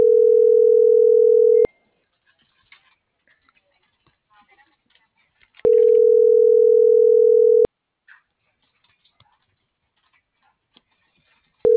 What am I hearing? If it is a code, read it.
Ringback tone (North America), 440 + 480 Hz.